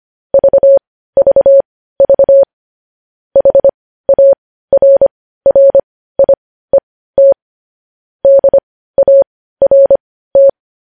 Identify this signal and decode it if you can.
Morse code, heard as a keyed tone, about 25 wpm, spaced as for 15 wpm, 550 Hz. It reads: VVV HARRIET DART
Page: news headlines in morse code at 15 wpm